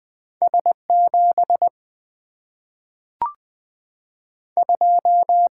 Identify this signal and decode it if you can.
Morse code, heard as a keyed tone, about 20 wpm, 700 Hz. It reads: S7 2
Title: Single-Letter Single-Number 20wpm